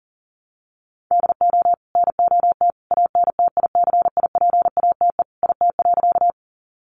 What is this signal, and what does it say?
Morse code, heard as a keyed tone, about 40 wpm, 700 Hz. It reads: DO NOT ANTICIPATE IT.
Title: Sentences from Top 1500 Words 40wpm